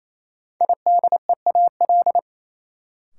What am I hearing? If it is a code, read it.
Morse code, heard as a keyed tone, about 28 wpm, 700 Hz. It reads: IDEAL
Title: Sentences from Top 1400 Words - New Word Drill Alt - 28wpm